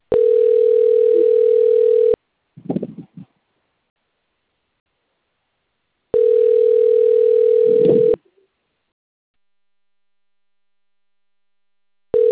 Ringback tone (North America), 440 + 480 Hz.